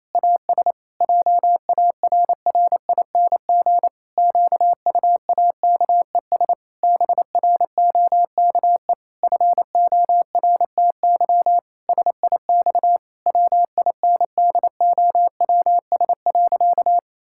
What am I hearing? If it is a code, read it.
Morse code, heard as a keyed tone, about 28 wpm, 700 Hz. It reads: AS JARRING QUAKES BROKE FORTY SIX WINDOWS.
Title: Pangram Challenge v2 - 28wpm